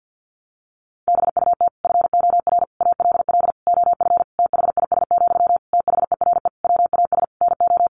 Morse code, heard as a keyed tone, about 50 wpm, 700 Hz. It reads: BUT FOR ALL OF THIS, THERE WAS NO
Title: Sentences from Top 1300 Words 50wpm